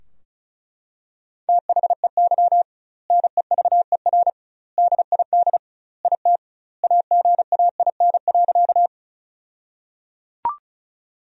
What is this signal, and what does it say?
Morse code, heard as a keyed tone, about 34 wpm, 700 Hz. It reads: THEY NEVER DID IT AGAIN.
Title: 2x Word-Spacing - Sentences from Top 300 Words - 35wpm